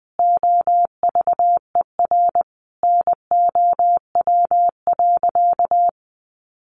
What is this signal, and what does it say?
Morse code, heard as a keyed tone, about 20 wpm, 700 Hz. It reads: OVER NOW.
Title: Sentences from Top 1300 Words 20wpm